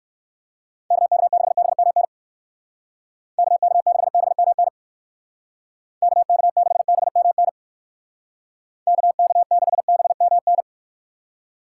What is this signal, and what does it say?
Morse code, heard as a keyed tone, about 59 wpm, 700 Hz. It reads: KK6BMN KK6BMN KK6BMN KK6BMN